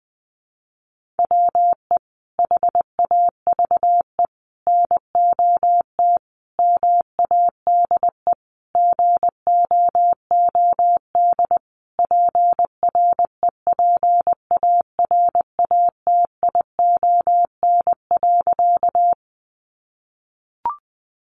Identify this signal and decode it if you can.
Morse code, heard as a keyed tone, about 20 wpm, 700 Hz. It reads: WE HAVE NOT MADE GOOD PREPARATION.
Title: Sentences from Top 1400 Words 20wpm